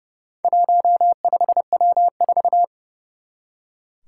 Morse code, heard as a keyed tone, about 30 wpm, 700 Hz. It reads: 15W4
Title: Four Challenging Letter-Number Combinations 30wpm